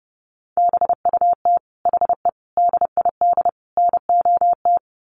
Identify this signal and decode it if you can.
Morse code, heard as a keyed tone, about 30 wpm, 700 Hz. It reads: BUT HE DID NOT
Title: Sets of 4 Words 30wpm